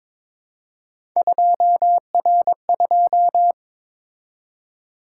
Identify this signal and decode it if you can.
Morse code, heard as a keyed tone, about 22 wpm, 700 Hz. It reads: 2R2
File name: practice / 22wpm / Three Letter-Number Combinations - 22wpm